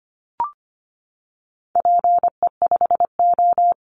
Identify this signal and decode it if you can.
Morse code, heard as a keyed tone, about 25 wpm, 700 Hz. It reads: PE5O
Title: Call Signs - 25wpm